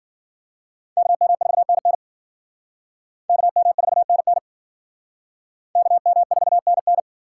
Morse code, heard as a keyed tone, about 52 wpm, 700 Hz. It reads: KM4NN KM4NN KM4NN